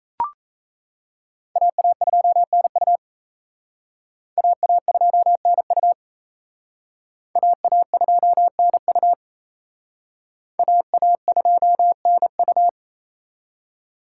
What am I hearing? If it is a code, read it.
Morse code, heard as a keyed tone, about 35 wpm, 700 Hz. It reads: AA2NU AA2NU AA2NU AA2NU
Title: Speed Racing - USA 2x2 Call Signs 28wpm